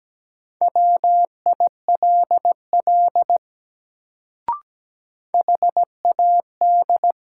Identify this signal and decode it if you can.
Morse code, heard as a keyed tone, about 17 wpm, 700 Hz. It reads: WILL HAD
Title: Sentences from Top 100 Words - New Word Drill - 17wpm